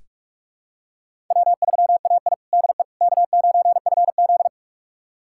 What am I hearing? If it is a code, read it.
Morse code, heard as a keyed tone, about 46 wpm, 700 Hz. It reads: W3AI DE K9FZ
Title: USA Callsign DE Callsign - 4 Characters - 45wpm